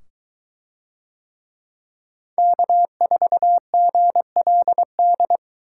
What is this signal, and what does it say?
Morse code, heard as a keyed tone, about 22 wpm, 700 Hz. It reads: K4GLD